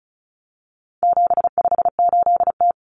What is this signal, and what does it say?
Morse code, heard as a keyed tone, about 34 wpm, 700 Hz. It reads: 758T
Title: Common Cut Numbers - 4-Digit 35wpm